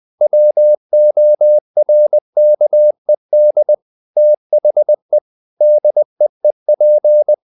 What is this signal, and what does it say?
Morse code, heard as a keyed tone, about 20 wpm, 600 Hz. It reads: WORKED THE DEEP
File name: the old man and the sea / the old man and the sea part 1 20wpm